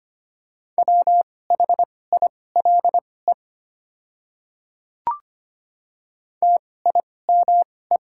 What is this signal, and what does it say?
Morse code, heard as a keyed tone, about 25 wpm, spaced as for 18 wpm, 700 Hz. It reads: WHILE TIME
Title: 2x Character Spacing - Top 300 Words - 25wpm